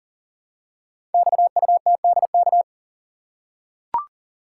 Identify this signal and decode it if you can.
Morse code, heard as a keyed tone, about 38 wpm, 690 Hz. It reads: XUTDK
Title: Code Groups - 5 Letters - 40wpm